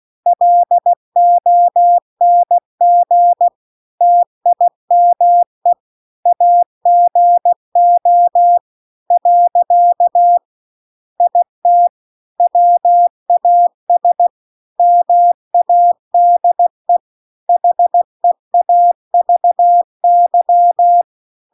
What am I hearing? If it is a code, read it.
Morse code, heard as a keyed tone, about 16 wpm, 700 Hz. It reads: LONG TIME AGO . IT WAS MADE HEAVY
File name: CW-The-Big-Twoheaded-River-2-16WPM